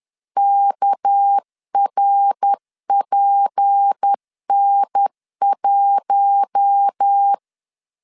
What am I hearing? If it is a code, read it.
Morse code, heard as a keyed tone, about 10 wpm, 800 Hz. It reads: KRPN1
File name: LEZIONI / 10 SIGNS / 10C DOT 10wpm 2940